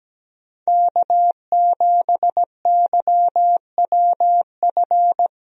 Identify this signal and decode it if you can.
Morse code, heard as a keyed tone, about 17 wpm, 700 Hz. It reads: K7YWF
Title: USA Call Signs - 5 Characters - Repeated 2x - 17wpm